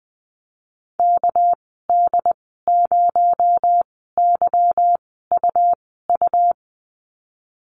Morse code, heard as a keyed tone, about 20 wpm, spaced as for 15 wpm, 700 Hz. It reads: KD0YUU
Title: 219 Lesson 35g - Number 0 - Call Signs 20wpm